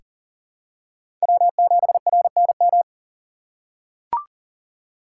Morse code, heard as a keyed tone, about 40 wpm, 700 Hz. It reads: W7RNM
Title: USA Call Signs - 5 Characters - 4x Character Spacing - 40wpm